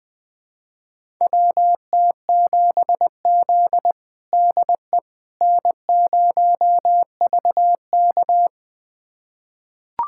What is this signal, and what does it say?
Morse code, heard as a keyed tone, about 20 wpm, 700 Hz. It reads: WT7Z DE N0VK
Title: USA Callsign DE Callsign - 4 Characters - 20wpm